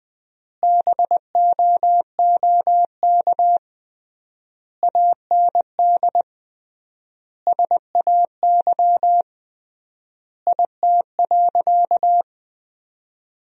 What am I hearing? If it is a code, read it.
Morse code, heard as a keyed tone, about 20 wpm, 700 Hz. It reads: BOOK AND SAY IT.
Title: 3x Word-Spacing - Sentences from Top 400 Words - 20wpm